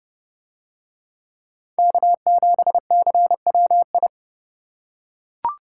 Morse code, heard as a keyed tone, about 28 wpm, 700 Hz. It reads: K7CWI